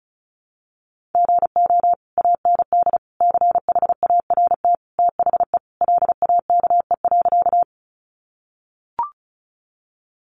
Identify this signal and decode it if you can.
Morse code, heard as a keyed tone, about 34 wpm, 700 Hz. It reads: GO AND CHART THE LAKE.
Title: Sentences from Top 1300 Words 35wpm